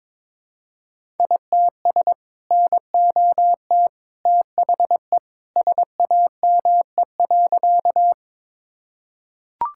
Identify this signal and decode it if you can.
Morse code, heard as a keyed tone, about 21 wpm, 700 Hz. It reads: ITS NOT THE SAME.
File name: practice / 22wpm / Sentences with 2-Word Phrases - 22wpm